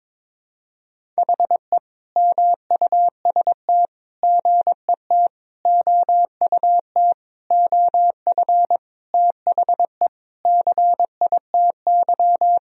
Morse code, heard as a keyed tone, about 22 wpm, 700 Hz. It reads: HE MUST GET OUT OF THE CITY
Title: Sentences from Top 1000 Words - Review Part 1 - 22wpm